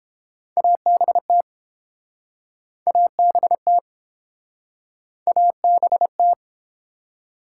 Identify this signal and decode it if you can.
Morse code, heard as a keyed tone, about 30 wpm, 700 Hz. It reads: ABT ABT ABT